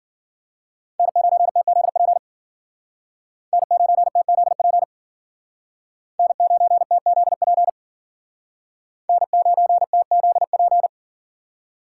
Morse code, heard as a keyed tone, about 52 wpm, 700 Hz. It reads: N9TZP N9TZP N9TZP N9TZP